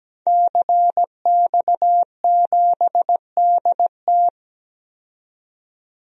Morse code, heard as a keyed tone, about 17 wpm, 700 Hz. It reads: CX7DT
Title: DXCC-100 Call Signs - 5 Characters - 17wpm